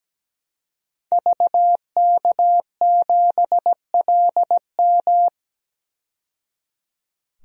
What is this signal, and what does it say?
Morse code, heard as a keyed tone, about 17 wpm, 700 Hz. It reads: VK7LM
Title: DXCC-100 Call Signs - 5 Characters - 17wpm